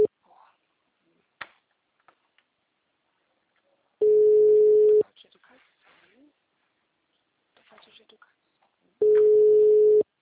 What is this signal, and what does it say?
Ringback tone (Europe, 425 Hz), with speech.
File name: out- -2003- - - .120